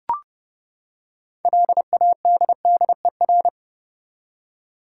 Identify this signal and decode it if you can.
Morse code, heard as a keyed tone, about 32 wpm, 700 Hz. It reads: LADDER
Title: Six Letter Words - 30wpm